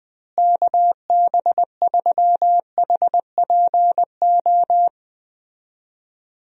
Morse code, heard as a keyed tone, about 20 wpm, 700 Hz. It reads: KB3HPO
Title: USA Call Signs - 6 Characters - Repeated 3x - 20wpm